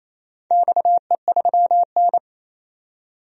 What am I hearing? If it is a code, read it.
Morse code, heard as a keyed tone, about 28 wpm, 700 Hz. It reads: XE3N
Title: DXCC-100 Call Signs - 4 Characters - 28wpm